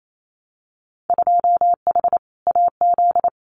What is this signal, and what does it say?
Morse code, heard as a keyed tone, about 28 wpm, 700 Hz. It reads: 2H AZ
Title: Winter Field Day Exchanges 28wpm